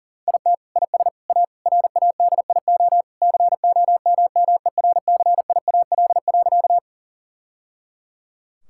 Morse code, heard as a keyed tone, about 40 wpm, 700 Hz. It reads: IT IS A RADIO COMMERCIAL.
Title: Sentences from Top 1000 Words - Review Part 5 - 40wpm